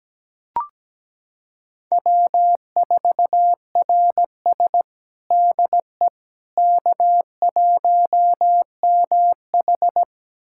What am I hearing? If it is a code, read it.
Morse code, heard as a keyed tone, about 17 wpm, 700 Hz. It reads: W4RS DE K1MH
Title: USA Callsign DE Callsign - 4 Characters - 17wpm